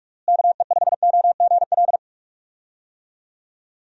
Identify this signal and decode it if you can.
Morse code, heard as a keyed tone, about 41 wpm, 700 Hz. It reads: KE5OGL
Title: USA Call Signs - 6 Characters - Repeated 2x - 45wpm